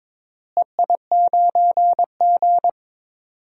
Morse code, heard as a keyed tone, about 22 wpm, 700 Hz. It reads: EI9G
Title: DXCC-100 Call Signs - 4 Characters - Repeated 2x - 22wpm